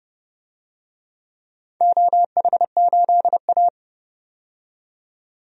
Morse code, heard as a keyed tone, about 30 wpm, 700 Hz. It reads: OH8A
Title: DXCC-100 Call Signs - 4 Characters - 30wpm